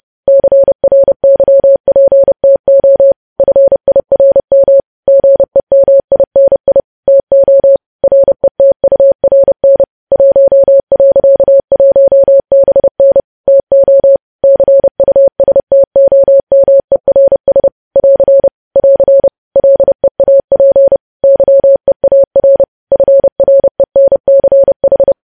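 Morse code, heard as a keyed tone, about 30 wpm, 550 Hz. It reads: CRYPTO FIRM GEMINI TO RETURN 1.1BN TO CUSTOMERS <AR> <AR> LEAP YEAR FRENCH
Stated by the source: Morse news headlines sent at 30 wpm.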